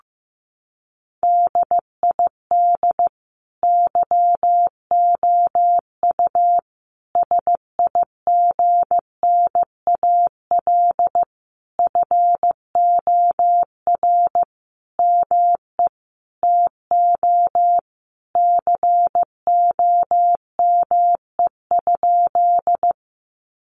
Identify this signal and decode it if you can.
Morse code, heard as a keyed tone, about 14 wpm, 700 Hz. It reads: DID YOU SIGNAL FOR ME TO COME?
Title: Sentences from Top 1400 Words 15wpm